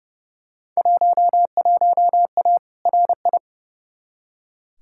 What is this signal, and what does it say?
Morse code, heard as a keyed tone, about 30 wpm, 700 Hz. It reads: 11A RI
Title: Field Day Exchanges Encore - 30wpm